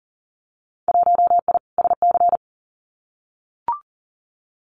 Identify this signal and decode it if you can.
Morse code, heard as a keyed tone, about 40 wpm, 700 Hz. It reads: 1I SC
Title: Winter Field Day Exchanges 40wpm